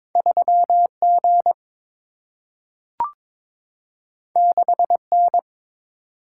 Morse code, heard as a keyed tone, about 22 wpm, 700 Hz. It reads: 3G 6N
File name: practice / 22wpm / Two Letter-Number Combinations - 22wpm